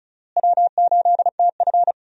Morse code, heard as a keyed tone, about 36 wpm, 700 Hz. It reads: W8TF
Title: USA Call Signs - 4 Characters - Repeated 3x - 35wpm